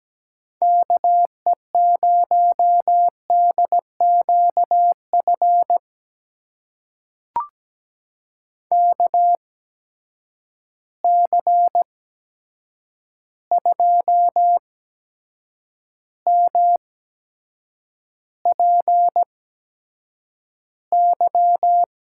Morse code, heard as a keyed tone, about 17 wpm, spaced as for 7 wpm, 700 Hz. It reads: KE0DQF KC2MPY
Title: USA Call Signs - 6 Characters - 8x Character Spacing - 17wpm